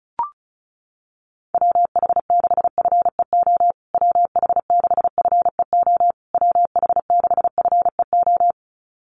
Morse code, heard as a keyed tone, about 34 wpm, 700 Hz. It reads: WH6FEO WH6FEO WH6FEO
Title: USA Call Signs Repeated 3x 35wpm